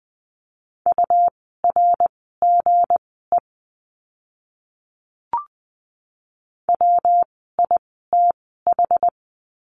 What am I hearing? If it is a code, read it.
Morse code, heard as a keyed tone, about 20 wpm, spaced as for 14 wpm, 700 Hz. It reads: URGE WITH
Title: 160 Lesson 26f - Letter g - Words 20wpm